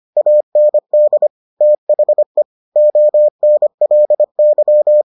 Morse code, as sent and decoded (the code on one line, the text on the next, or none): .- -. -.. / - .... / --- -. .-.. -.--
AND THE ONLY